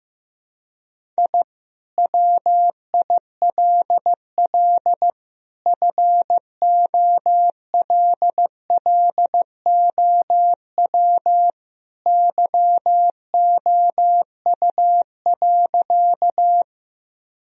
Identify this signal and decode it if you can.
Morse code, heard as a keyed tone, about 16 wpm, 700 Hz. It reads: I WILL FOLLOW YOU.
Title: Sentences from Top 1000 Words - Review Part 1 - 15wpm